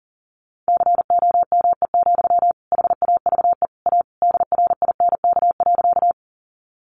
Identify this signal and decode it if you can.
Morse code, heard as a keyed tone, about 40 wpm, 700 Hz. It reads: COME, HAVE A DRINK.
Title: Sentences from Top 1500 Words 40wpm